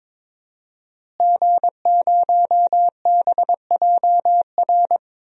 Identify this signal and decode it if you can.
Morse code, heard as a keyed tone, about 22 wpm, 700 Hz. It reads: G0BJR
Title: DXCC-100 Call Signs - 5 Characters - Repeated 2x - 22wpm